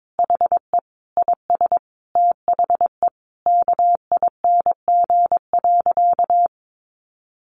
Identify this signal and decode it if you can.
Morse code, heard as a keyed tone, about 22 wpm, 700 Hz. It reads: HE IS THE KING.
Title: Sentences from Top 1200 Words 22wpm